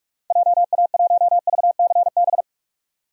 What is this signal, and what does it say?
Morse code, heard as a keyed tone, about 47 wpm, 700 Hz. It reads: JA1VCB
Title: DXCC-100 Call Signs - 6 Characters - 45wpm